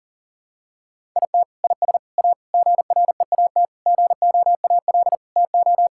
Morse code, heard as a keyed tone, about 40 wpm, 700 Hz. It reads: IT IS A GREAT GOAL TO
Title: Sentences from Top 1000 Words - Review Part 5 - 40wpm